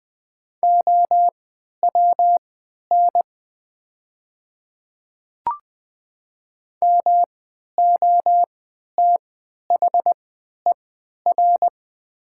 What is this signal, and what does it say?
Morse code, heard as a keyed tone, about 20 wpm, spaced as for 11 wpm, 700 Hz. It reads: OWN MOTHER
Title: 3x Character Spacing - Top 300 Words - 20wpm